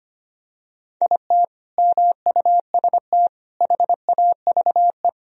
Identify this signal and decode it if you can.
Morse code, heard as a keyed tone, about 25 wpm, 700 Hz. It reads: IT MUST HAVE
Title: Sentences from Top 1000 Words - Review Part 1 - 25wpm